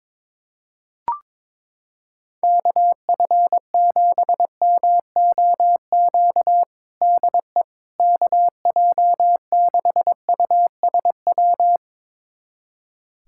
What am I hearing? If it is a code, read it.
Morse code, heard as a keyed tone, about 22 wpm, 700 Hz. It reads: KF7MOQ DE KJ6USW
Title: USA Callsign DE Callsign - 6 Characters - 22wpm